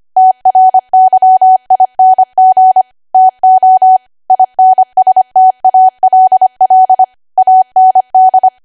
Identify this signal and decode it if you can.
Morse code, heard as a keyed tone, about 25 wpm, 750 Hz. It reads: TRYING TO INSTALL AND
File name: Archive / 25 WPM / 25WPM